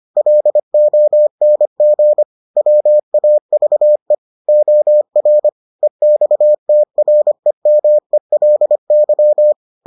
Morse code, heard as a keyed tone, about 25 wpm, 600 Hz. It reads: LONG WAVE OR EXTREMELY